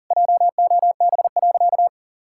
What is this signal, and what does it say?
Morse code, heard as a keyed tone, about 40 wpm, 700 Hz. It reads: JOB.
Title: Sentences from Top 1600 Words 40wpm